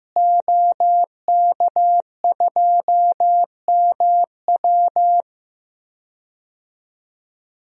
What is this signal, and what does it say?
Morse code, heard as a keyed tone, about 15 wpm, 700 Hz. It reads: OK2MW
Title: DXCC-100 Call Signs - 5 Characters - Repeated 3x - 15wpm